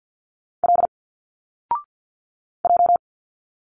Morse code, heard as a keyed tone, about 48 wpm, 700 Hz. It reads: L J
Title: Single Letters 50wpm